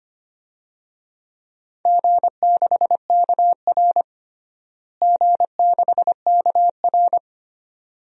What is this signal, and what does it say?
Morse code, heard as a keyed tone, about 25 wpm, 700 Hz. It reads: G6KR G6KR
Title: DXCC-100 Call Signs - 4 Characters - Repeated 2x - 25wpm